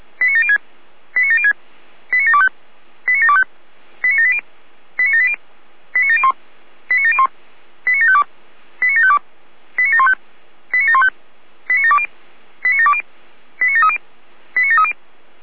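Five-tone selective call (ZVEI), heard as quick runs of five tones, tones of about 70 ms. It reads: 89706 89706 89725 89725 89780 89780 89701 89701 89752 89752 89715 89715 89720 89720 89730 89730
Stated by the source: ZVEI five-tone selective call received on an SDR.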